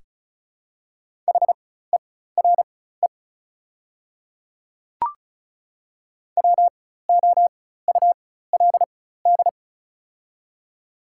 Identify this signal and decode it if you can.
Morse code, heard as a keyed tone, about 36 wpm, spaced as for 16 wpm, 700 Hz. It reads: HERE WOULD
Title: 4x Character Spacing - Top 200 Words - 35wpm